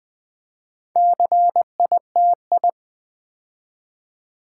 Morse code, heard as a keyed tone, about 20 wpm, 700 Hz. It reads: CITI
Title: Build A Word - 4 Letters At A Time - 11-Letter Words - 20wpm